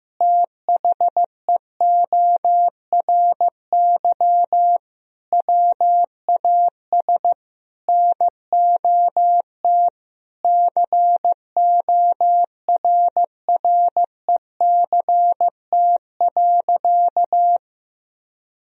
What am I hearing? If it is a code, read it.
Morse code, heard as a keyed tone, about 15 wpm, 700 Hz. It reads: THEORY WAS NOT CORRECT.
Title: Sentences from Top 1000 Words - Review Part 3 - 15wpm